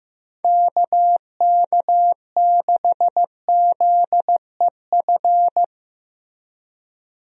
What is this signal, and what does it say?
Morse code, heard as a keyed tone, about 15 wpm, 700 Hz. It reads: KK6ZEF
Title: USA Call Signs - 6 Characters - Repeated 3x - 15wpm